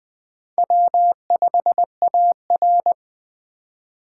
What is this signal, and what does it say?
Morse code, heard as a keyed tone, about 19 wpm, 700 Hz. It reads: W5AR